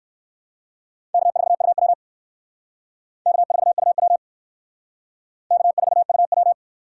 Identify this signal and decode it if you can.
Morse code, heard as a keyed tone, about 57 wpm, 700 Hz. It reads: K4UW K4UW K4UW